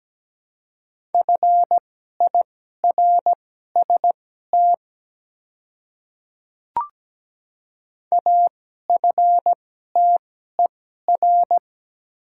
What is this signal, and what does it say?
Morse code, heard as a keyed tone, about 17 wpm, spaced as for 12 wpm, 690 Hz. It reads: FIRST AFTER
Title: 2x Character Spacing - Top 300 Words - 17wpm